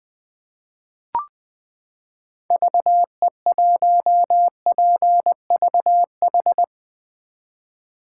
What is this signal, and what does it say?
Morse code, heard as a keyed tone, about 20 wpm, 700 Hz. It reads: VE1PVH
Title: 2023 SOTA Chaser Call Signs - 20wpm